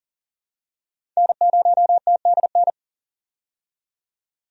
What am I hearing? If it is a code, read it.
Morse code, heard as a keyed tone, about 40 wpm, 700 Hz. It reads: N0TDN
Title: USA Call Signs - 5 Characters - 40wpm